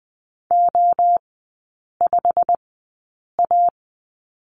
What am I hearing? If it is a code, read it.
Morse code, heard as a keyed tone, about 20 wpm, spaced as for 8 wpm, 690 Hz. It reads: O5A
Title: 087 Lesson 15e - Number 5 - Three Characters 20wpm